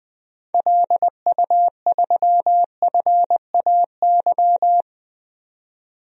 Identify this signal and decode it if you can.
Morse code, heard as a keyed tone, about 21 wpm, 700 Hz. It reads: LU3FAY
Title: DXCC-100 Call Signs - 6 Characters - 20wpm